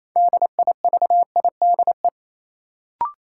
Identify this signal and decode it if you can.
Morse code, heard as a keyed tone, about 28 wpm, 700 Hz. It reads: DIVIDE
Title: Sentences from Top 1600 Words - New Word Drill - 28wpm